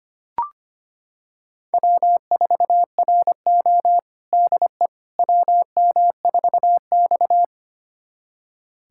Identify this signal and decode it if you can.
Morse code, heard as a keyed tone, about 25 wpm, 700 Hz. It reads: W4RO DE WM4X
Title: USA Callsign DE Callsign - 4 Characters - 25wpm